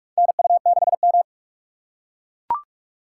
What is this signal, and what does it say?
Morse code, heard as a keyed tone, about 43 wpm, 700 Hz. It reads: NU6M
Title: USA Call Signs - 4 Characters - 3x Character Spacing - 45wpm